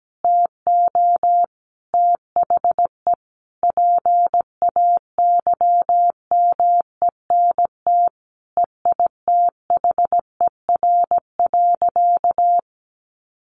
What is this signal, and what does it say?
Morse code, heard as a keyed tone, about 17 wpm, 700 Hz. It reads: TO THE PAYMENT EITHER.
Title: Sentences from Top 1200 Words 17wpm